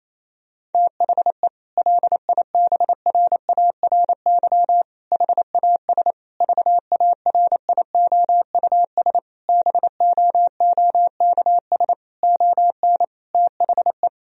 Morse code, heard as a keyed tone, about 27 wpm, 700 Hz. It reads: THE LIBRARY HAS VARIOUS BOOKS ON THE
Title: Sentences from Top 1000 Words - Review Part 4 - 28wpm